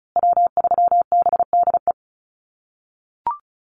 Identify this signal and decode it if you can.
Morse code, heard as a keyed tone, about 34 wpm, 700 Hz. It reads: W3BDE
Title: USA Call Signs Repeated 2x 35wpm